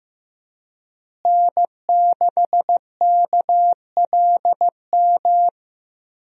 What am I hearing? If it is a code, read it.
Morse code, heard as a keyed tone, about 15 wpm, 700 Hz. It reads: N6KLM
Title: USA Call Signs - 5 Characters - Repeated 2x - 15wpm